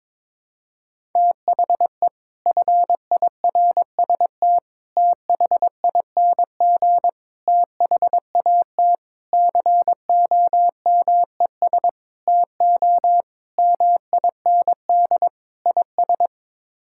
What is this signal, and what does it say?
Morse code, heard as a keyed tone, about 21 wpm, 700 Hz. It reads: THE FIRST THING THAT COMES TO MIND IS
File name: practice / 22wpm / Sets of 8 Words - 22wpm